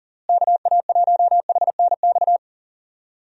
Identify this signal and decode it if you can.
Morse code, heard as a keyed tone, about 40 wpm, 700 Hz. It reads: KA1HNX
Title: USA Call Signs - 6 Characters - Repeated 2x - 40wpm